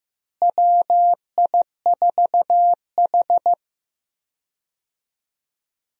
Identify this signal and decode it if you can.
Morse code, heard as a keyed tone, about 15 wpm, 700 Hz. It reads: WI4H
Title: USA Call Signs - 4 Characters - Repeated 2x - 15wpm